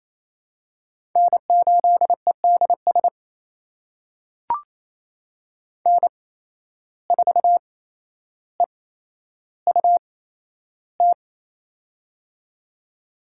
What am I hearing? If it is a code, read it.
Morse code, heard as a keyed tone, about 27 wpm, spaced as for 12 wpm, 700 Hz. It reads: N8EDS N4EUT
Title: USA Call Signs - 5 Characters - 8x Character Spacing - 28wpm